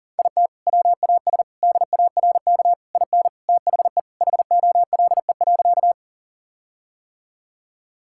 Morse code, heard as a keyed tone, about 37 wpm, 700 Hz. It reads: IT WAS DARK IN THE HOLE.
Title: Sentences from Top 1600 Words 40wpm